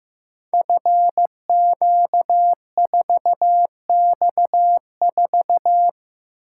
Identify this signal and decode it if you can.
Morse code, heard as a keyed tone, about 15 wpm, 700 Hz. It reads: FQ4X4
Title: Code Groups - 5 Letter:Numbers - 15wpm